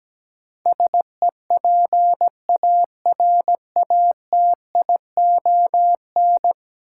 Morse code, heard as a keyed tone, about 17 wpm, 700 Hz. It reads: SEPARATION
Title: Sets of 5 Words - New Word Drill - 17wpm